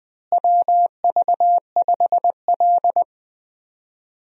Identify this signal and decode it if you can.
Morse code, heard as a keyed tone, about 20 wpm, 700 Hz. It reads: WV5L
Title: USA Call Signs - 4 Characters - Repeated 2x - 20wpm